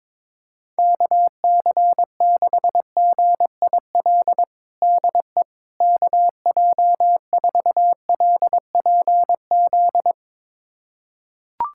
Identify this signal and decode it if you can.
Morse code, heard as a keyed tone, about 22 wpm, 700 Hz. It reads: KC6GIL DE KJ4LPZ
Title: USA Callsign DE Callsign - 6 Characters - 22wpm